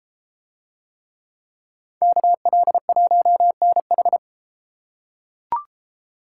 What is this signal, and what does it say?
Morse code, heard as a keyed tone, about 33 wpm, 700 Hz. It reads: KL1NH